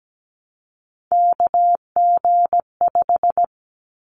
Morse code, heard as a keyed tone, about 17 wpm, 700 Hz. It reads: KG5
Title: Callsign Prefixes for Top 25 DXCC - 17wpm